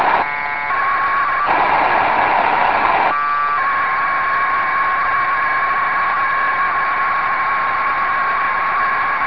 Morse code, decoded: none